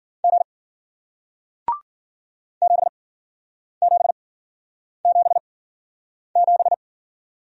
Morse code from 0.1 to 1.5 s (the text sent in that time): G